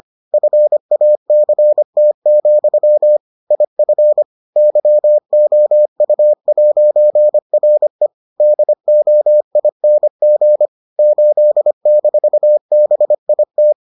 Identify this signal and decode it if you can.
Morse code, heard as a keyed tone, about 25 wpm, 600 Hz. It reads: FACT, IF YOU'RE DOING 8-BIT